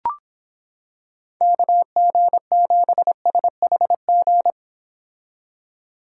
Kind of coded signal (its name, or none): Morse code